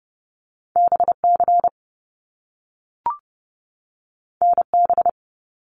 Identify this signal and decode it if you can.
Morse code, heard as a keyed tone, about 30 wpm, 700 Hz. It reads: BC NB
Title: Canadian Provinces and Territories 30wpm